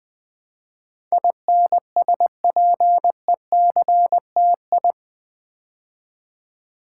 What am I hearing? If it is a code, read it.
Morse code, heard as a keyed tone, about 20 wpm, 700 Hz. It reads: INSPECTI
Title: Build A Word - 4 Letters At A Time - 10-Letter Words - 20wpm